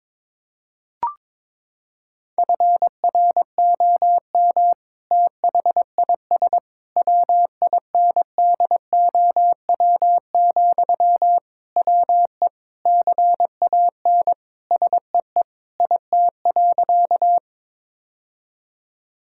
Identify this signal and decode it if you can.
Morse code, heard as a keyed tone, about 21 wpm, 700 Hz. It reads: FROM THIS WINDOW, WE CAN SEE IT.
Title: Sentences from Top 1600 Words 22wpm